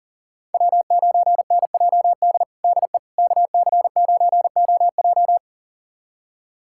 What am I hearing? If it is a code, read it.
Morse code, heard as a keyed tone, about 40 wpm, 700 Hz. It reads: W9NJD DE KC9OJ
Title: USA Callsign DE Callsign - 5 Characters - 40wpm